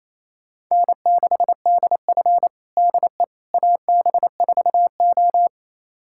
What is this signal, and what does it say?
Morse code, heard as a keyed tone, about 28 wpm, 700 Hz. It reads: N6DF DE AB4O
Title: USA Callsign DE Callsign - 4 Characters - 28wpm